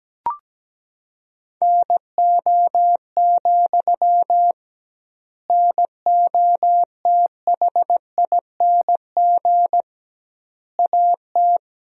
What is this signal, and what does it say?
Morse code, heard as a keyed tone, about 17 wpm, 700 Hz. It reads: NO, NOTHING AT
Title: 2x Word-Spacing - Sentences from Top 500 Words - 17wpm